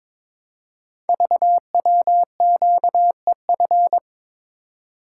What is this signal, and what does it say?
Morse code, heard as a keyed tone, about 22 wpm, 700 Hz. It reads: VWQEF
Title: Code Groups - 5 Letters - 22wpm